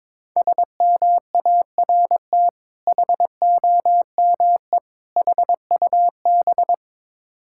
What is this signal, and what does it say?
Morse code, heard as a keyed tone, about 22 wpm, 690 Hz. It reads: SMART HOME HUB
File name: Difficult Words with Hints - 22wpm